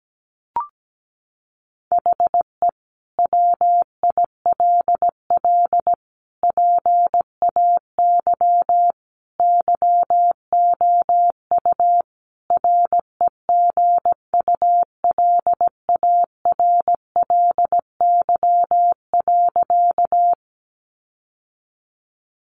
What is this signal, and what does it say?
Morse code, heard as a keyed tone, about 17 wpm, 700 Hz. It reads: HE WILL PAY YOU REGULARLY.
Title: Sentences from Top 1400 Words 17wpm